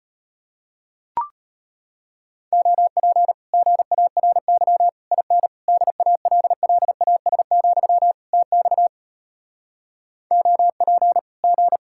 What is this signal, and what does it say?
Morse code, heard as a keyed tone, about 38 wpm, 700 Hz. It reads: OP GARY IN DALLAS, TX OP G